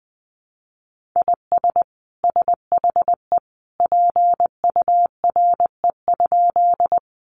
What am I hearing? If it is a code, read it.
Morse code, heard as a keyed tone, about 20 wpm, 700 Hz. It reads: IS SHE PURE?
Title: Sentences from Top 1500 Words 20wpm